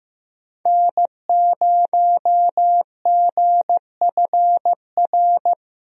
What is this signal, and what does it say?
Morse code, heard as a keyed tone, about 15 wpm, 700 Hz. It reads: N0GFR
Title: USA Call Signs - 5 Characters - 15wpm